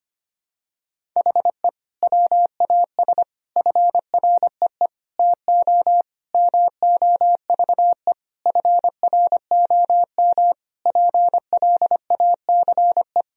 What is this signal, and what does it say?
Morse code, heard as a keyed tone, about 25 wpm, 700 Hz. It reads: HE WAS FREE TO MOVE FROM PLACE
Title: Sentences from Top 1000 Words - Review Part 1 - 25wpm